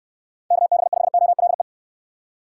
Morse code, heard as a keyed tone, about 69 wpm, 700 Hz. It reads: KB5OLE